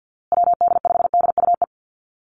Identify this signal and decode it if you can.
Morse code, heard as a keyed tone, about 50 wpm, 700 Hz. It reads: WD5DUE
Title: USA Call Signs Repeated 3x 50wpm